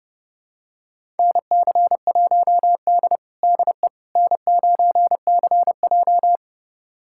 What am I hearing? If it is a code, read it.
Morse code, heard as a keyed tone, about 32 wpm, 700 Hz. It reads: NC1D DE N9CJ
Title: USA Callsign DE Callsign - 4 Characters - 30wpm